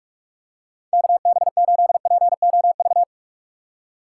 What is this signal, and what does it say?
Morse code, heard as a keyed tone, about 46 wpm, 700 Hz. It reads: KB8POV
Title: USA Call Signs - 6 Characters - 3x Character Spacing - 45wpm